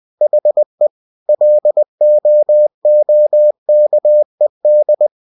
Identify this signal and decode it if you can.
Morse code, heard as a keyed tone, about 20 wpm, 600 Hz. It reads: HE LOOKED
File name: cw-books / winnie-the-pooh / winnie-20wpm-page-0099